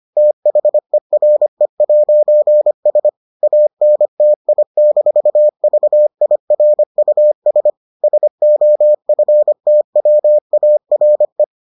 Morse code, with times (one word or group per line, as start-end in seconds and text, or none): 0.2-3.1 THERE'S
3.4-7.7 ANTI-VIRUS
8.0-11.4 SOFTWARE